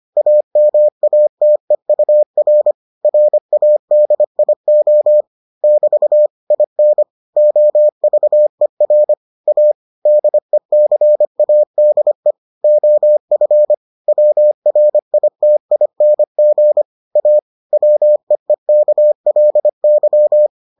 Morse code, heard as a keyed tone, about 25 wpm, 600 Hz. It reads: AMATEUR RADIO <BT> IN OVER A DECADE OF WRITING A WEEKLY